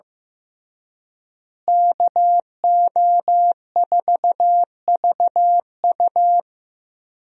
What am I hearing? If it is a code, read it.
Morse code, heard as a keyed tone, about 15 wpm, 700 Hz. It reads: KO4VU
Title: USA Call Signs - 5 Characters - Repeated 2x - 15wpm